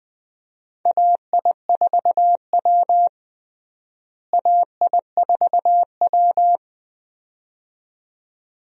Morse code, heard as a keyed tone, about 21 wpm, 700 Hz. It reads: AI4W AI4W
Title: USA Call Signs - 4 Characters - Repeated 2x - 20wpm